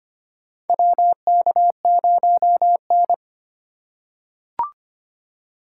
Morse code, heard as a keyed tone, about 25 wpm, 700 Hz. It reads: WK0N
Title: USA Call Signs - 4 Characters - 4x Character Spacing - 25wpm